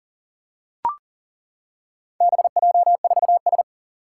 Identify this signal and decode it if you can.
Morse code, heard as a keyed tone, about 40 wpm, 700 Hz. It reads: BJ4S
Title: Four Challenging Letter-Number Combinations 40wpm